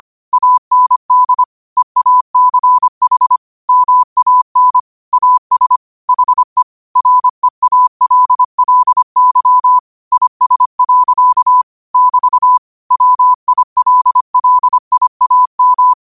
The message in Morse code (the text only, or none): AND EACH MAN AS HE REALLY IS. <BT> WILLIAM